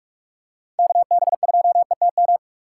Morse code, heard as a keyed tone, about 41 wpm, 700 Hz. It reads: KB2ETM
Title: USA Call Signs - 6 Characters - 3x Character Spacing - 45wpm